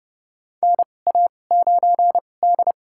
Morse code, heard as a keyed tone, about 30 wpm, spaced as for 22 wpm, 700 Hz. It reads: NA9D
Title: USA Call Signs - 4 Characters - 2x Character Spacing - 30wpm